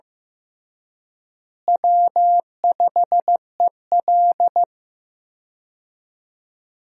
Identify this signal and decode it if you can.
Morse code, heard as a keyed tone, about 15 wpm, 700 Hz. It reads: W5EL
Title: USA Call Signs - 4 Characters - Repeated 3x - 15wpm